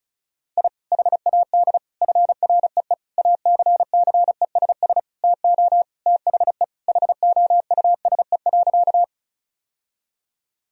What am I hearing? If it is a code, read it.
Morse code, heard as a keyed tone, about 36 wpm, 700 Hz. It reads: I HAD FREE ACCESS TO THE HOUSE.
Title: Sentences from Top 1000 Words - Review Part 4 - 35wpm